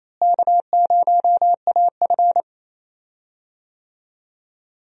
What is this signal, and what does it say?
Morse code, heard as a keyed tone, about 28 wpm, 700 Hz. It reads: K0AF
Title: USA Call Signs - 4 Characters - Repeated 2x - 28wpm